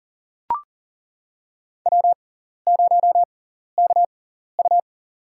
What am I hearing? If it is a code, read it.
Morse code, heard as a keyed tone, about 40 wpm, spaced as for 14 wpm, 700 Hz. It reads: W0KU
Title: USA Call Signs - 6x Character Spacing - 40wpm